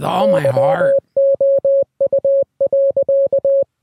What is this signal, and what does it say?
Morse code, heard as a keyed tone, about 20 wpm, 550 Hz, with speech. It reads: YOU.